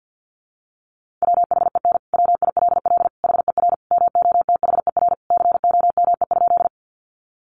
Morse code, heard as a keyed tone, about 48 wpm, 700 Hz. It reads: WHEN WILL HER MOTHER COME?
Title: Sentences from Top 200 Words 50wpm